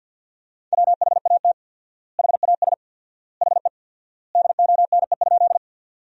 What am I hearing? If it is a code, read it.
Morse code, heard as a keyed tone, about 50 wpm, 700 Hz. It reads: WHAT HAS HE DONE?
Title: 4x Word-Spacing - Sentences from Top 500 Words - 50wpm